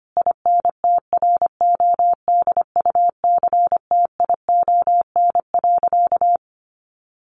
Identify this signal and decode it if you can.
Morse code, heard as a keyed tone, about 25 wpm, 700 Hz. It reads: INTRODUCTION.
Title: Sentences from Top 1000 Words 25wpm